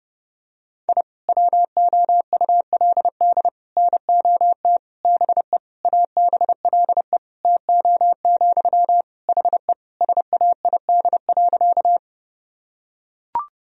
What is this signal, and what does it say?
Morse code, heard as a keyed tone, about 30 wpm, 700 Hz. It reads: I WOULD NOT BE ABLE TO, HE SAID.
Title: Sentences from Top 1000 Words - Review Part 2 - 30wpm